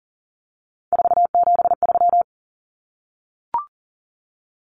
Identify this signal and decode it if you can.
Morse code, heard as a keyed tone, about 40 wpm, 700 Hz. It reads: 473
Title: Numbers - 3-Digit 40wpm